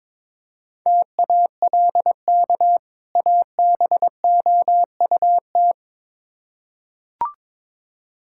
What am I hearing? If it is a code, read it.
Morse code, heard as a keyed tone, about 22 wpm, 700 Hz. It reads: TALK ABOUT
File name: Build A Phrase - 2-Word Phrases - 22wpm